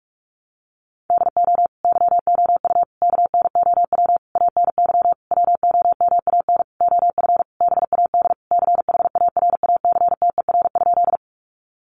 Morse code, heard as a keyed tone, about 44 wpm, 700 Hz. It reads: DO YOU KNOW ANY WOMAN OF BAD CHARACTER?
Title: Sentences from Top 1200 Words 45wpm